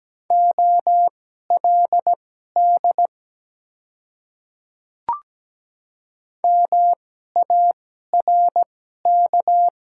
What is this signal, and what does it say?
Morse code, heard as a keyed tone, about 17 wpm, spaced as for 12 wpm, 700 Hz. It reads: OLD MARK
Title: 2x Character Spacing - Top 300 Words - 17wpm